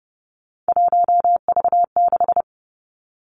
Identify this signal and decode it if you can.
Morse code, heard as a keyed tone, about 30 wpm, 700 Hz. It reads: 1V6
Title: Three Challenging Letter-Number Combinations 30wpm